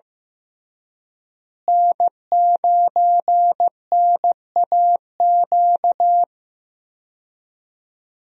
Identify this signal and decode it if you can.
Morse code, heard as a keyed tone, about 16 wpm, 700 Hz. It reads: N9NAQ
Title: USA Call Signs - 5 Characters - Repeated 3x - 15wpm